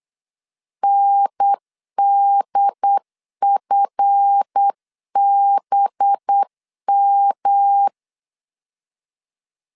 Morse code, heard as a keyed tone, about 8 wpm, 800 Hz. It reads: NDFBM
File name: LEZIONI / 7 GF9Z / 7 GF 8wpm 3108